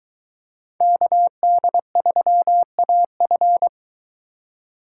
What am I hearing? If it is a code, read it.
Morse code, heard as a keyed tone, about 23 wpm, 700 Hz. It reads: KD3AF